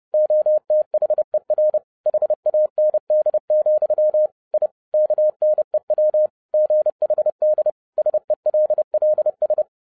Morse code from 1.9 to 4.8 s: HAND, I